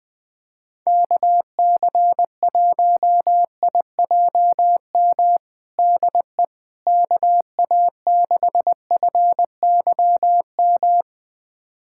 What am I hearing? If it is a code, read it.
Morse code, heard as a keyed tone, about 20 wpm, 700 Hz. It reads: KC1IJM DE KA6FYM
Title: USA Callsign DE Callsign - 6 Characters - 20wpm